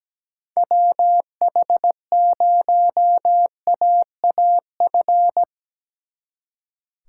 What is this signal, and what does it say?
Morse code, heard as a keyed tone, about 18 wpm, 700 Hz. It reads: WH0AAF
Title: Speed Racing - USA Call Signs 17wpm